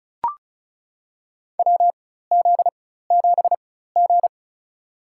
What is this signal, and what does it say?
Morse code, heard as a keyed tone, about 36 wpm, spaced as for 16 wpm, 700 Hz. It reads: WZ7G
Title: USA Call Signs - 4 Characters - 4x Character Spacing - 35wpm